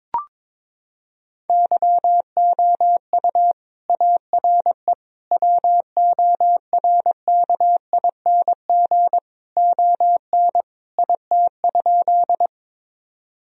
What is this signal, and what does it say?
Morse code, heard as a keyed tone, about 22 wpm, 700 Hz. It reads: YOU ARE WORKING ON IT?
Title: Sentences from Top 1000 Words - Review Part 4 - 22wpm